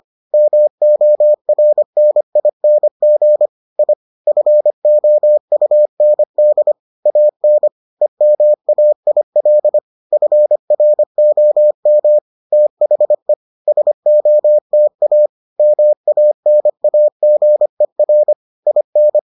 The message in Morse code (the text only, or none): MORNING I FOUND AN EMAIL FROM THE SOTA MANAGER IN